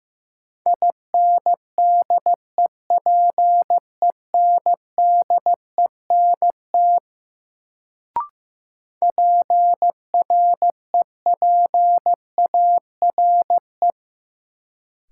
Morse code, heard as a keyed tone, about 15 wpm, 700 Hz. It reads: INDEPENDENT PREPARE
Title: Sentences from Top 900 Words - New Word Drill - 15wpm